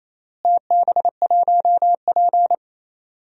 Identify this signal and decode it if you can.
Morse code, heard as a keyed tone, about 27 wpm, 700 Hz. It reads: TB1P
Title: DXCC-100 Call Signs - 4 Characters - Repeated 3x - 28wpm